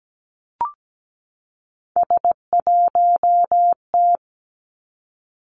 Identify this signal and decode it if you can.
Morse code, heard as a keyed tone, about 17 wpm, 700 Hz. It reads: S1T
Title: Three Challenging Letter-Number Combinations 17wpm